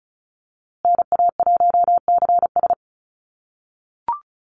Morse code, heard as a keyed tone, about 34 wpm, 700 Hz. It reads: NA1CS
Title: USA Call Signs Repeated 2x 35wpm